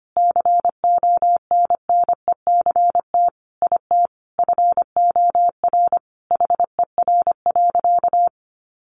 Morse code, heard as a keyed tone, about 25 wpm, 700 Hz. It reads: CONNECT IT FOR HER.
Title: Sentences from Top 1500 Words 25wpm